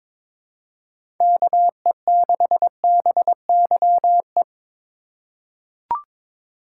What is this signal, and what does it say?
Morse code, heard as a keyed tone, about 22 wpm, 700 Hz. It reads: KE6BYE
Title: USA Call Signs - 6 Characters - 4x Character Spacing - 22wpm